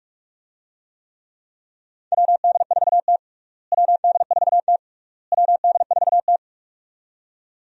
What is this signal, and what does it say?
Morse code, heard as a keyed tone, about 46 wpm, 700 Hz. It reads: WD4T WD4T WD4T
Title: USA Call Signs - 4 Characters - Repeated 3x - 45wpm